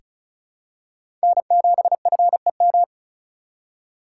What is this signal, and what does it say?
Morse code, heard as a keyed tone, about 36 wpm, 700 Hz. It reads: N7FEM
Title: USA Call Signs - 5 Characters - 4x Character Spacing - 35wpm